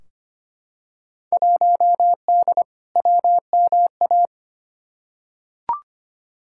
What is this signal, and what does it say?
Morse code, heard as a keyed tone, about 25 wpm, 700 Hz. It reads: 1D WMA
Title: Field Day Exchanges - 25wpm